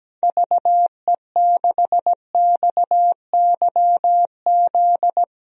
Morse code, heard as a keyed tone, about 17 wpm, 700 Hz. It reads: VE6XYZ
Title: DXCC-100 Call Signs - 6 Characters - Repeated 3x - 17wpm